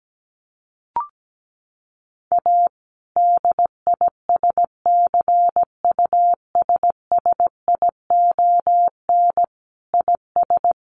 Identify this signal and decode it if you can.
Morse code, heard as a keyed tone, about 17 wpm, 700 Hz. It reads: A DISCUSSION IS
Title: Sentences from Top 1100 Words 17wpm